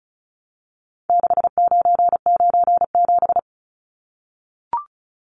Morse code, heard as a keyed tone, about 34 wpm, 700 Hz. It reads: 6997
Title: Numbers - 4-Digit 35wpm